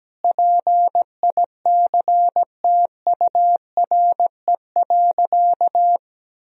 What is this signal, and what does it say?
Morse code, heard as a keyed tone, about 17 wpm, 700 Hz. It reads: PICTURE.
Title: Sentences from Top 1000 Words - Review Part 1 - 17wpm